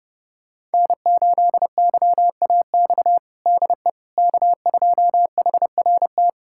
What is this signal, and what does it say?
Morse code, heard as a keyed tone, about 28 wpm, 700 Hz. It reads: N8YAX DE K2HRT
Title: USA Callsign DE Callsign - 5 Characters - 30wpm